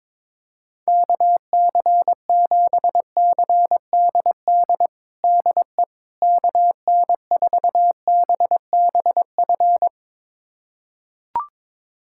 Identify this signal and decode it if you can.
Morse code, heard as a keyed tone, about 22 wpm, 700 Hz. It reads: KC7CDD DE KN4BBF
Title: USA Callsign DE Callsign - 6 Characters - 22wpm